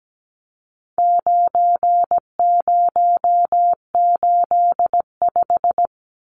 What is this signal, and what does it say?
Morse code, heard as a keyed tone, about 17 wpm, 700 Hz. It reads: 9085
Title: Numbers - 4-Digit 17wpm